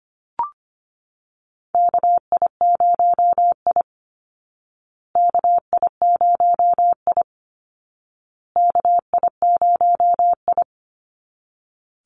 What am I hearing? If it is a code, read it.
Morse code, heard as a keyed tone, about 25 wpm, 700 Hz. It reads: KI0I KI0I KI0I
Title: Call Signs Repeated 2x - 25wpm